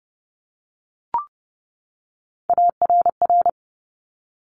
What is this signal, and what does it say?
Morse code, heard as a keyed tone, about 30 wpm, 700 Hz. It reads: ARR
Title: Top Three Letter Combinations in Words 30wpm